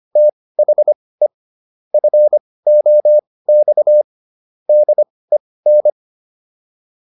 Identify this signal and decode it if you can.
Morse code, heard as a keyed tone, about 25 wpm, spaced as for 18 wpm, 600 Hz. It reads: THE FOX DEN